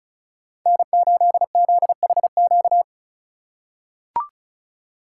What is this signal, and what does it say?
Morse code, heard as a keyed tone, about 34 wpm, 700 Hz. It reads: N8ZHQ
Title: USA Call Signs - 3x Character Spacing - 35wpm